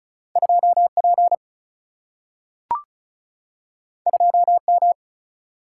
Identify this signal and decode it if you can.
Morse code, heard as a keyed tone, about 36 wpm, 700 Hz. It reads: 2 P 2 M
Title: Single-Letter Single-Number 35wpm